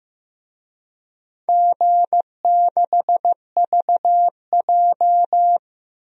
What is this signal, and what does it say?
Morse code, heard as a keyed tone, about 15 wpm, 700 Hz. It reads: G6VJ
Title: DXCC-100 Call Signs - 4 Characters - 15wpm